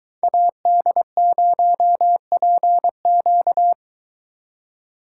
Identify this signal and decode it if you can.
Morse code, heard as a keyed tone, about 23 wpm, 700 Hz. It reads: AD0PQ